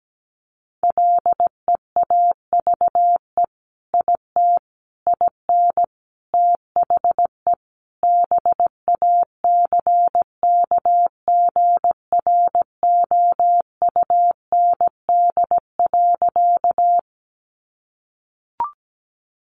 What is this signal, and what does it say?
Morse code, heard as a keyed tone, about 17 wpm, 700 Hz. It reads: LEAVE IT IN THE BACKGROUND.
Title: Sentences from Top 1300 Words 17wpm